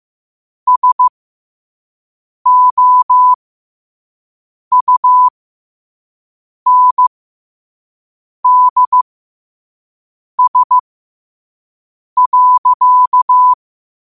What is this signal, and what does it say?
Morse code, heard as a keyed tone, about 15 wpm, spaced as for 5 wpm, 1000 Hz. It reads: SOUNDS.